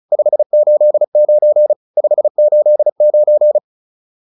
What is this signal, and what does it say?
Morse code, heard as a keyed tone, about 34 wpm, 600 Hz. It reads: 589 589
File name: QSO403 35